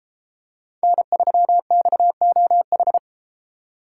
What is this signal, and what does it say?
Morse code, heard as a keyed tone, about 33 wpm, 700 Hz. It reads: N3XOH